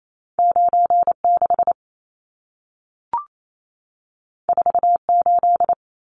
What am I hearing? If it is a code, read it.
Morse code, heard as a keyed tone, about 28 wpm, 700 Hz. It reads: 96 48
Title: Numbers - 2-Digit 28wpm